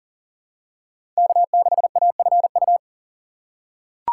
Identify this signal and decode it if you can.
Morse code, heard as a keyed tone, about 40 wpm, 700 Hz. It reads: K6AFU
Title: USA Call Signs - 5 Characters - 4x Character Spacing - 40wpm